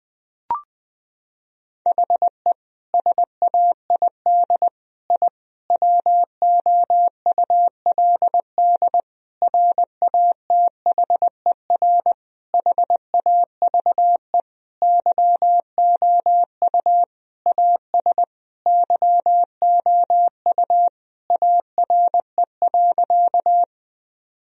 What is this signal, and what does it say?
Morse code, heard as a keyed tone, about 21 wpm, 700 Hz. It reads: HE SAID I WOULD RATHER HAVE YOU AS YOU ARE.
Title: Sentences from Top 1000 Words - Review Part 4 - 20wpm